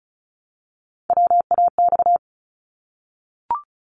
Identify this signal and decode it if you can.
Morse code, heard as a keyed tone, about 35 wpm, 700 Hz. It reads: WAX
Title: Multi-letter Phonemes Repeated 3x 35wpm